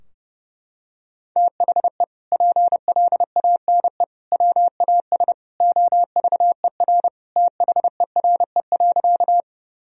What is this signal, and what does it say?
Morse code, heard as a keyed tone, about 30 wpm, 700 Hz. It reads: THE PLANE WAS OVER THERE.
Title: Sentences from Top 1000 Words - Review Part 3 - 30wpm